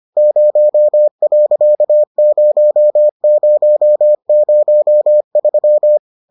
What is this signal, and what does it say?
Morse code, heard as a keyed tone, about 25 wpm, 600 Hz. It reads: 0.0003